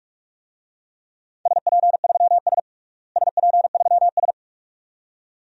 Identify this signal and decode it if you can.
Morse code, heard as a keyed tone, about 47 wpm, 700 Hz. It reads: SP3S SP3S
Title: DXCC-100 Call Signs - 4 Characters - Repeated 2x - 45wpm